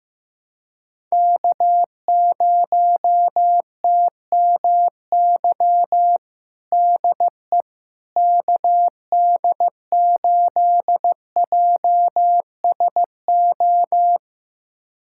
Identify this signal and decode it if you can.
Morse code, heard as a keyed tone, about 15 wpm, 700 Hz. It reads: K0TMY DE KD8JSO
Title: USA Callsign DE Callsign 15wpm